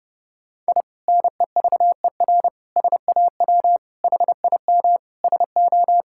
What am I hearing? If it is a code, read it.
Morse code, heard as a keyed tone, about 30 wpm, 700 Hz. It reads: I NEVER SAW HIM SO
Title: Sentences from Top 1000 Words - Review Part 6 - 30wpm